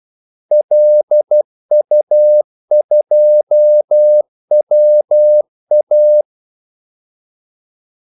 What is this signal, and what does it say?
Morse code, heard as a keyed tone, about 12 wpm, 600 Hz. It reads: LU2WA
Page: 2 ekstra ordmellemrum mellem kaldesignalerne